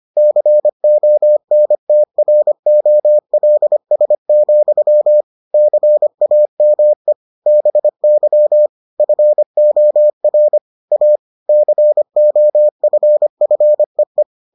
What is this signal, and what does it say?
Morse code, heard as a keyed tone, about 25 wpm, 600 Hz. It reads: CONTROLS, CAME BY FOR A COFFEE